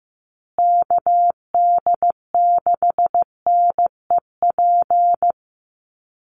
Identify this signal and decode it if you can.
Morse code, heard as a keyed tone, about 15 wpm, 700 Hz. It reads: KD6NEP
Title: USA Call Signs Repeated 3x 15wpm